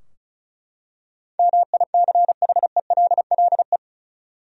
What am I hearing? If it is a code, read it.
Morse code, heard as a keyed tone, about 34 wpm, 700 Hz. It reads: MICHELLE
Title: USA Common Names - 35wpm